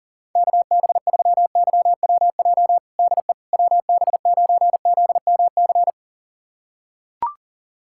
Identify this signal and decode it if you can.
Morse code, heard as a keyed tone, about 40 wpm, 700 Hz. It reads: KB3YWJ DE WB9ZMC
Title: USA Callsign DE Callsign 40wpm